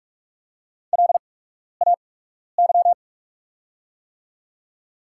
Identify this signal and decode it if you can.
Morse code, heard as a keyed tone, about 48 wpm, spaced as for 12 wpm, 690 Hz. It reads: LAY
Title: 8x Character Spacing - Top 500 Words - 45wpm